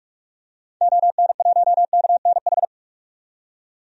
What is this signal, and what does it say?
Morse code, heard as a keyed tone, about 46 wpm, 700 Hz. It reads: ON1KNH
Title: DXCC-100 Call Signs - 6 Characters - 45wpm